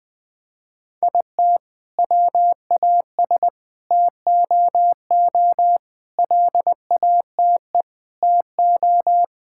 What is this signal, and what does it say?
Morse code, heard as a keyed tone, about 20 wpm, 700 Hz. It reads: IT WAS TOO LATE TO
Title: Sentences from Top 1600 Words 20wpm